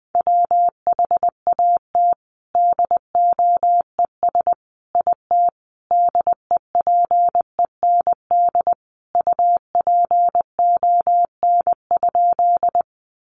Morse code, heard as a keyed tone, about 20 wpm, 700 Hz. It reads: WHAT DOES IT DEPEND UPON?
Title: Sentences from Top 1300 Words 20wpm